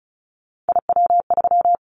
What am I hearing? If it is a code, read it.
Morse code, heard as a keyed tone, about 36 wpm, 700 Hz. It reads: IW3
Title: Callsign Prefixes for Top 50 DXCC - 35wpm